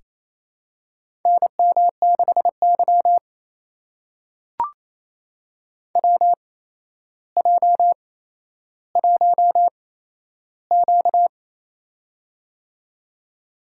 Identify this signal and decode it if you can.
Morse code, heard as a keyed tone, about 28 wpm, spaced as for 12 wpm, 700 Hz. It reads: NM6Y WJ1Q
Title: USA Call Signs - 4 Characters - 8x Character Spacing - 28wpm